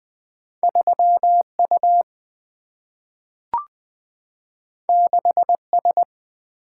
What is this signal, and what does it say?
Morse code, heard as a keyed tone, about 20 wpm, 700 Hz. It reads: 3U 6S
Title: Single-Letter Single-Number 20wpm